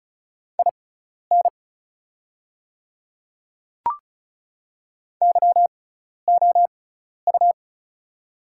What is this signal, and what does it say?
Morse code, heard as a keyed tone, about 36 wpm, spaced as for 12 wpm, 700 Hz. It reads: IN YOU
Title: 6x Character Spacing - Top 100 Words - 35wpm